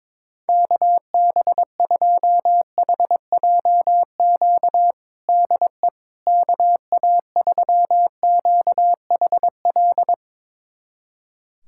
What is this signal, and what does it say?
Morse code, heard as a keyed tone, about 21 wpm, 700 Hz. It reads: KB2HJQ DE KA3QHL
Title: USA Callsign DE Callsign 22wpm